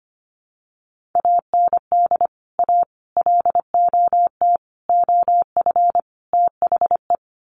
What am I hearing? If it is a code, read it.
Morse code, heard as a keyed tone, about 25 wpm, 700 Hz. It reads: AND A LOT OF THE
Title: Sets of 5 Words Encore 25wpm